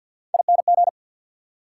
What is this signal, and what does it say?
Morse code, heard as a keyed tone, about 53 wpm, 690 Hz. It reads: ING
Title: Prefixes and Suffixes 50wpm